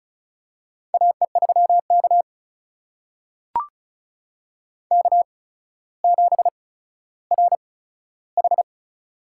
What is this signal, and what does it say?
Morse code, heard as a keyed tone, about 34 wpm, spaced as for 15 wpm, 700 Hz. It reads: AE3K K7RH
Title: USA Call Signs - 4 Characters - 8x Character Spacing - 35wpm